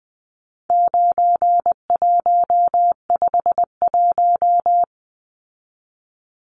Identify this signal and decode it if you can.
Morse code, heard as a keyed tone, about 20 wpm, 700 Hz. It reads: 9151
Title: Numbers - 4-Digit 20wpm